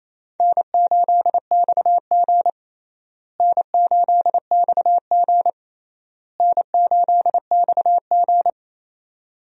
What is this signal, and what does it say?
Morse code, heard as a keyed tone, about 28 wpm, 700 Hz. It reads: N8XG N8XG N8XG
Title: USA Call Signs - 4 Characters - Repeated 3x - 28wpm